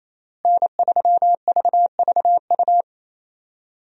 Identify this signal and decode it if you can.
Morse code, heard as a keyed tone, about 28 wpm, 700 Hz. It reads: N3VVU
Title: USA Call Signs - 5 Characters - Repeated 3x - 28wpm